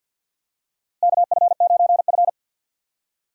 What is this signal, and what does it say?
Morse code, heard as a keyed tone, about 53 wpm, 700 Hz. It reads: KF9F
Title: USA Call Signs - 4 Characters - Repeated 2x - 50wpm